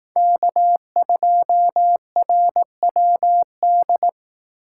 Morse code, heard as a keyed tone, about 18 wpm, 700 Hz. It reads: K2RWD
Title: Speed Racing - USA 1x3 Call Signs 15wpm